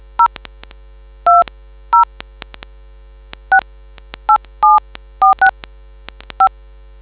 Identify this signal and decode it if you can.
DTMF, heard as telephone touch tones, tones of about 100 ms. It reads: 020 687465